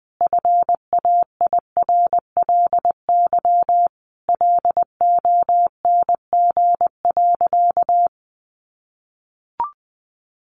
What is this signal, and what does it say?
Morse code, heard as a keyed tone, about 20 wpm, 700 Hz. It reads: FAIRLY LONG.
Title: Sentences from Top 1100 Words 20wpm